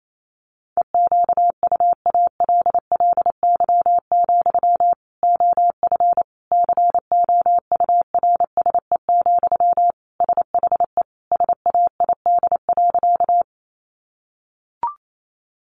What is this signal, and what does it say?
Morse code, heard as a keyed tone, about 28 wpm, 700 Hz. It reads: EQUALLY, OF COURSE, SHE SAID.
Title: Sentences from Top 1400 Words 28wpm